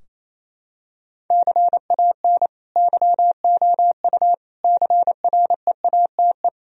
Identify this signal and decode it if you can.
Morse code, heard as a keyed tone, about 27 wpm, 700 Hz. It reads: CAN YOU CREATE
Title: Sentences from Top 1000 Words - Review Part 3 - 28wpm